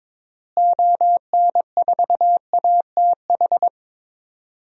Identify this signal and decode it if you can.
Morse code, heard as a keyed tone, about 22 wpm, 700 Hz. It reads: ON4ATH
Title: DXCC-100 Call Signs - 6 Characters - Repeated 2x - 22wpm